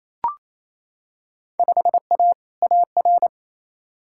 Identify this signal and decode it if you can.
Morse code, heard as a keyed tone, about 28 wpm, 700 Hz. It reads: 5A AR
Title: Field Day Exchanges Encore - 28wpm